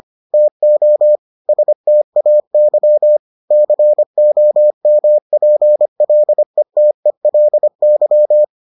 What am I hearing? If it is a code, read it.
Morse code, heard as a keyed tone, about 25 wpm, 600 Hz. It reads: TO STAY COMPLETELY